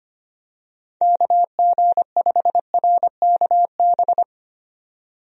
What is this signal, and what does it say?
Morse code, heard as a keyed tone, about 25 wpm, 700 Hz. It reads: KG5RKB
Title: USA Call Signs - 6 Characters - 25wpm